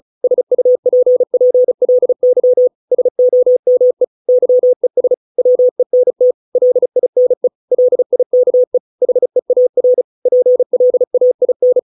Morse code, heard as a keyed tone, about 34 wpm, 500 Hz. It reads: SUPPLY SOME YES WENT LINE LIKE HEAR PLAIN